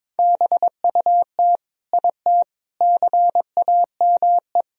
Morse code, heard as a keyed tone, about 21 wpm, 700 Hz. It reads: BUT IT CAME
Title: Sentences from Top 1000 Words - Review Part 2 - 22wpm